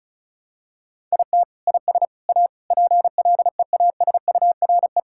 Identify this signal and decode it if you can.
Morse code, heard as a keyed tone, about 36 wpm, 700 Hz. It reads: IT IS A PLEASURE
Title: Sentences from Top 1600 Words 35wpm